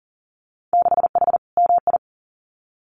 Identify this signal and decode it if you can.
Morse code, heard as a keyed tone, about 40 wpm, 700 Hz. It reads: 6H MI
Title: Winter Field Day Exchanges Encore 40wpm